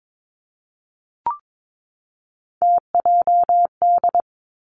Morse code, heard as a keyed tone, about 22 wpm, 700 Hz. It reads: TJD
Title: Three Challenging Letter-Number Combinations 22wpm